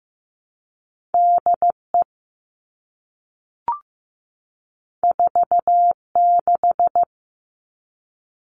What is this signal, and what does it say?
Morse code, heard as a keyed tone, about 15 wpm, 700 Hz. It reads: DE 46
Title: All Cut Numbers - 2-Digit 15wpm